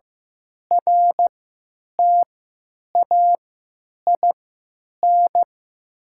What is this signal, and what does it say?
Morse code, heard as a keyed tone, about 15 wpm, spaced as for 9 wpm, 700 Hz. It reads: RTAIN
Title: 3x Character Spacing - Top 500 Words - 15wpm